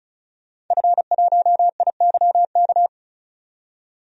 Morse code, heard as a keyed tone, about 36 wpm, 700 Hz. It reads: F1IYK
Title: DXCC-100 Call Signs - 5 Characters - 35wpm